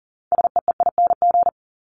Morse code, heard as a keyed tone, about 40 wpm, 700 Hz. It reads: SEEING
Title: ING Suffix - Top 100 Words with Base Words 40wpm